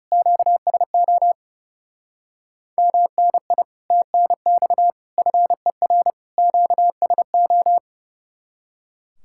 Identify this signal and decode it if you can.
Morse code, heard as a keyed tone, about 36 wpm, 700 Hz. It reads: QSO MNI TNX FER QSO
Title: Speed-Racing Common QSO Phrases Encore - 30wpm